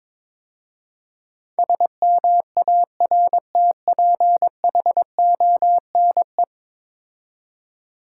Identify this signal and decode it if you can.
Morse code, heard as a keyed tone, about 22 wpm, 700 Hz. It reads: SMARTPHONE
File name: practice / 22wpm / 10 Letter Words - First 3 Letters Given - 22wpm